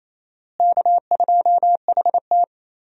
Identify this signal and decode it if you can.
Morse code, heard as a keyed tone, about 28 wpm, 700 Hz. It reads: K2HT
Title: USA Call Signs - 4 Characters - Repeated 3x - 28wpm